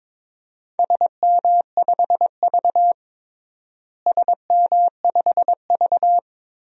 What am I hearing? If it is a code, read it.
Morse code, heard as a keyed tone, about 22 wpm, 700 Hz. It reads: SM5V SM5V
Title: DXCC-100 Call Signs - 4 Characters - Repeated 2x - 22wpm